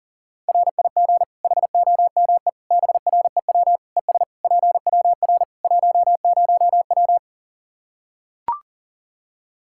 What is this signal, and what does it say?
Morse code, heard as a keyed tone, about 37 wpm, 700 Hz. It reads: RIG HOME BREW ES PWR 10W
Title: Practice QSO - Exchange 2 - Info - 40wpm